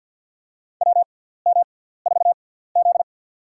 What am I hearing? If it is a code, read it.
Morse code, heard as a keyed tone, about 52 wpm, spaced as for 17 wpm, 700 Hz. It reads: WM4Z
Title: USA Call Signs - 4 Characters - 6x Character Spacing - 50wpm